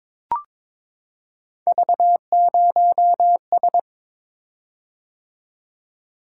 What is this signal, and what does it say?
Morse code, heard as a keyed tone, about 22 wpm, 700 Hz. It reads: V0S
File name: practice / 22wpm / Three Letter-Number Combinations - 22wpm